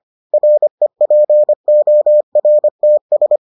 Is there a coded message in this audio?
Yes: Morse code from 0.3 to 3.4 s, heard as a keyed tone, about 25 wpm, 600 Hz. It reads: REPORTS